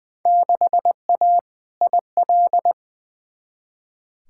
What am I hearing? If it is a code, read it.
Morse code, heard as a keyed tone, about 20 wpm, 700 Hz. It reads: 6A IL
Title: Field Day Exchanges - 20wpm